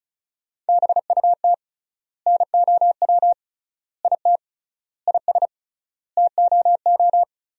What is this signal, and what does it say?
Morse code, heard as a keyed tone, about 34 wpm, 700 Hz. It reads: BUT NOW IT IS TOO
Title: 3x Word-Spacing - Sentences from Top 300 Words - 35wpm